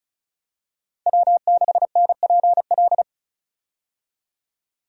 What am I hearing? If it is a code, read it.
Morse code, heard as a keyed tone, about 34 wpm, 700 Hz. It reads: W6NPL
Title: USA Call Signs - 5 Characters - Repeated 3x - 35wpm